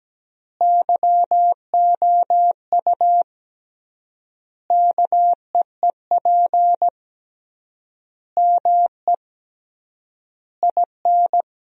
Morse code, heard as a keyed tone, about 17 wpm, 700 Hz. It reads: YOU KEEP ME IN
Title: 3x Word-Spacing - Sentences from Top 500 Words - 17wpm